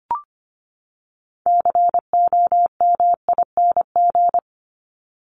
Morse code, heard as a keyed tone, about 25 wpm, 700 Hz. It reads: COMING
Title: ING Suffix - Top 100 Words with Base Words 25wpm